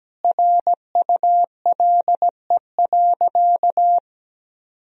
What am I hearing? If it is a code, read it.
Morse code, heard as a keyed tone, about 17 wpm, 700 Hz. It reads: RULE.
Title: Sentences from Top 1000 Words - Review Part 2 - 17wpm